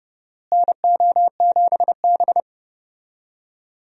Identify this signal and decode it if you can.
Morse code, heard as a keyed tone, about 32 wpm, 700 Hz. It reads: NO7B
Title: USA Call Signs - 4 Characters - Repeated 2x - 30wpm